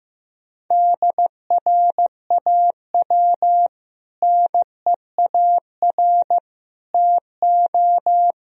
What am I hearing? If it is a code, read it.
Morse code, heard as a keyed tone, about 15 wpm, 700 Hz. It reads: DRAW NEAR TO
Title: Sentences from Top 1000 Words - Review Part 1 - 15wpm